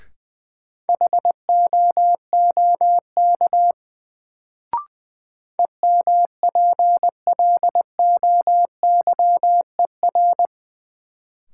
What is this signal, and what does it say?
Morse code, heard as a keyed tone, about 20 wpm, 700 Hz. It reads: HOOK EMPLOYER
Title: Sentences from Top 1600 Words - New Word Drill - 20wpm